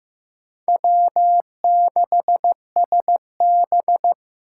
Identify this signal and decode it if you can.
Morse code, heard as a keyed tone, about 15 wpm, 700 Hz. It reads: W6SB
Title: USA Call Signs - 4 Characters - 4x Character Spacing - 15wpm